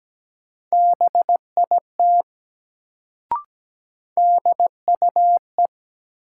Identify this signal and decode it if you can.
Morse code, heard as a keyed tone, about 17 wpm, 700 Hz. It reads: BIT DUE
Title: Sentences from Top 900 Words - New Word Drill - 17wpm